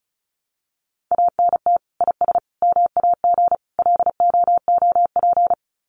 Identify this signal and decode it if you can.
Morse code, heard as a keyed tone, about 34 wpm, 700 Hz. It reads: ANT IS MAG LOOP
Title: Common QSO Phrases 35wpm